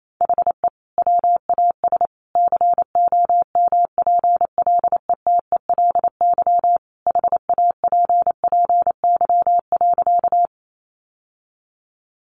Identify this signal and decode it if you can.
Morse code, heard as a keyed tone, about 28 wpm, 700 Hz. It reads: HE WAS COMPLETELY HAPPY.
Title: Sentences from Top 1100 Words 28wpm